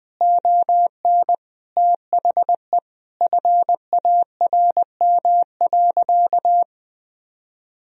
Morse code, heard as a keyed tone, about 20 wpm, 700 Hz. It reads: ON THE FARM.
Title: Sentences from Top 1000 Words - Review Part 2 - 20wpm